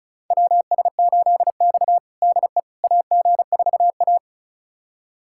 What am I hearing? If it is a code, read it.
Morse code, heard as a keyed tone, about 34 wpm, 700 Hz. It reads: WS8X DE AG4A
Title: USA Callsign DE Callsign - 4 Characters - 35wpm